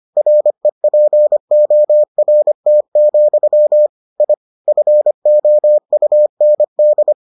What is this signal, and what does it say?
Morse code, heard as a keyed tone, about 25 wpm, 600 Hz. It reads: REPORT, I FOUND